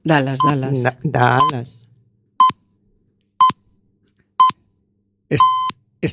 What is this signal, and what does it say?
Radio time signal (time pips), with speech. It is heard from 0.4 to 5.7 s.